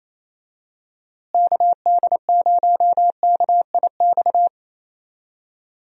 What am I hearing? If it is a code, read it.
Morse code, heard as a keyed tone, about 28 wpm, 700 Hz. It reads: KD0KIX
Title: USA Call Signs - 6 Characters - Repeated 3x - 28wpm